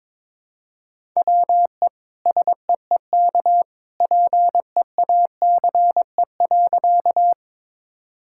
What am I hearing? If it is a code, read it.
Morse code, heard as a keyed tone, about 22 wpm, 700 Hz. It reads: WE SEEK PEACE.
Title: Sentences from Top 1600 Words 22wpm